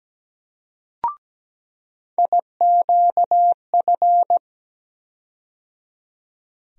Morse code, heard as a keyed tone, about 17 wpm, 700 Hz. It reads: IQF
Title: USA Call Sign Postfixes - 3 Letters 17wpm